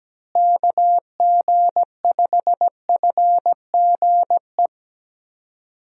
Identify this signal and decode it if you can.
Morse code, heard as a keyed tone, about 17 wpm, 700 Hz. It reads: KG5FGE
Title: USA Call Signs - 6 Characters - Repeated 2x - 17wpm